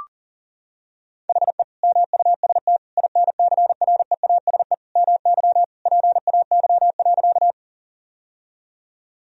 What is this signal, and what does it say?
Morse code, heard as a keyed tone, about 37 wpm, 700 Hz. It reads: HE MUST INCREASE MY PAY.
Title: Sentences from Top 1000 Words - Review Part 4 - 40wpm